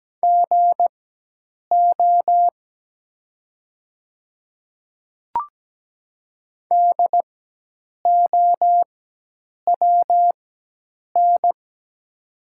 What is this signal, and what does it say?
Morse code, heard as a keyed tone, about 17 wpm, spaced as for 8 wpm, 700 Hz. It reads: GO DOWN
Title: 4x Character Spacing - Top 100 Words - 17wpm